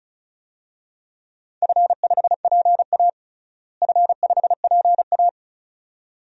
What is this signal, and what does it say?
Morse code, heard as a keyed tone, about 36 wpm, 700 Hz. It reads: F5PA F5PA
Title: DXCC-100 Call Signs - 4 Characters - Repeated 2x - 35wpm